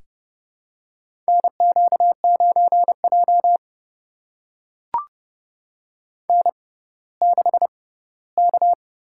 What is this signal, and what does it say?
Morse code, heard as a keyed tone, about 30 wpm, spaced as for 17 wpm, 700 Hz. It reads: NQ9J N6K
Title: USA Call Signs - 6x Character Spacing - 30wpm